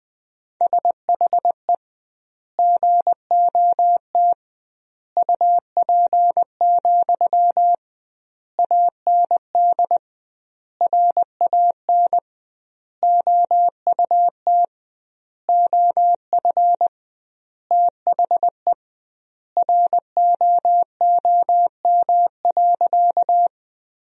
Morse code, heard as a keyed tone, about 20 wpm, 700 Hz. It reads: SHE GOT UP, AND RAN OUT OF THE ROOM.
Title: 2x Word-Spacing - Sentences from Top 500 Words - 20wpm